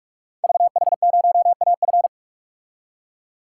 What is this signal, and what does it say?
Morse code, heard as a keyed tone, about 44 wpm, 690 Hz. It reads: VH0AF
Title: Code Groups - 5 Letter:Numbers - 45wpm